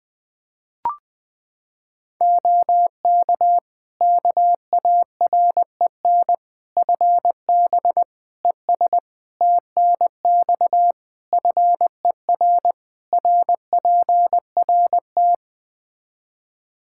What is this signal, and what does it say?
Morse code, heard as a keyed tone, about 20 wpm, 700 Hz. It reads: OK KAREN FB ES TNX FER RPRT
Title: Practice QSO - Exchange 2 - Info - 20wpm